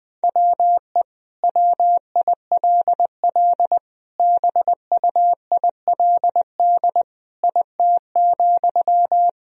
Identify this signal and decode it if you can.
Morse code, heard as a keyed tone, about 20 wpm, 700 Hz. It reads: WE WILL BUILD IT,
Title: Sentences from Top 1000 Words - Review Part 1 - 20wpm